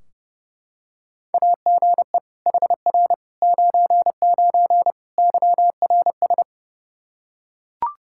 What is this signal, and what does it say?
Morse code, heard as a keyed tone, about 30 wpm, 700 Hz. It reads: AGE HR 99 YRS
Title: Practice QSO - Exchange 3 - More Info - 30wpm